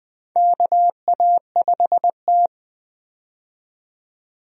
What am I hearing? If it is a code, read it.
Morse code, heard as a keyed tone, about 20 wpm, 700 Hz. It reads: KA5T
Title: USA Call Signs - 4 Characters - 20wpm